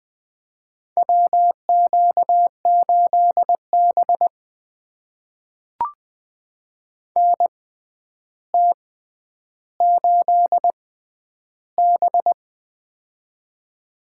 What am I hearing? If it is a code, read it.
Morse code, heard as a keyed tone, about 20 wpm, spaced as for 10 wpm, 700 Hz. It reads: WQ8B NT8B
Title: USA Call Signs - 4 Characters - 6x Character Spacing - 20wpm